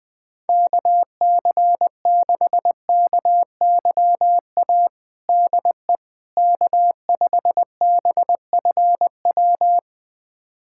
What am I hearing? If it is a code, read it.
Morse code, heard as a keyed tone, about 20 wpm, 700 Hz. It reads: KC6KYA DE K5BFW
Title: USA Callsign DE Callsign 20wpm